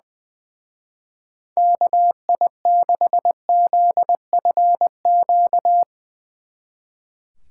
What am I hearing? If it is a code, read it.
Morse code, heard as a keyed tone, about 20 wpm, 700 Hz. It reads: KI6ZFQ
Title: Speed Racing - USA 2x3 Call Signs 20wpm